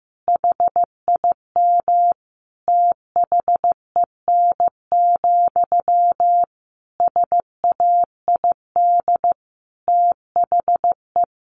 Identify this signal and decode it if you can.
Morse code, heard as a keyed tone, about 15 wpm, 700 Hz. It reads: HIM THEN, SAID THE
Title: Sentences from Top 1100 Words 15wpm